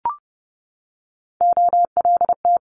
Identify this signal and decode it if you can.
Morse code, heard as a keyed tone, about 30 wpm, 700 Hz. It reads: OLT
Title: Multi-letter Phonemes 30wpm